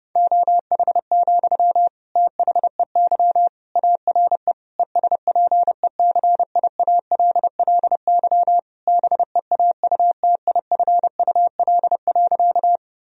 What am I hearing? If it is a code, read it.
Morse code, heard as a keyed tone, about 30 wpm, 700 Hz. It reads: OH, THEY ARE ESPECIALLY BEAUTIFUL.
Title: Sentences from Top 1000 Words - Review Part 4 - 30wpm